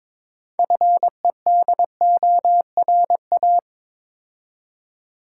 Morse code, heard as a keyed tone, about 23 wpm, 690 Hz. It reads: FEDORA
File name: Difficult Words with Hints - 22wpm